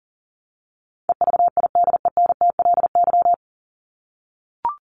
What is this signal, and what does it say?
Morse code, heard as a keyed tone, about 40 wpm, 700 Hz. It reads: EVIDENTLY
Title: LY Suffix - Top 100 Words with Base Words 40wpm